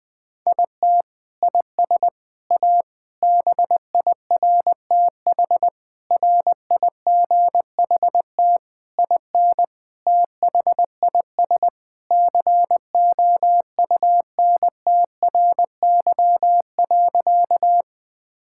Morse code, heard as a keyed tone, about 20 wpm, 700 Hz. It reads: IT IS A BIRTH RIGHT IN THIS COUNTRY.
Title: Sentences from Top 1000 Words - Review Part 5 - 20wpm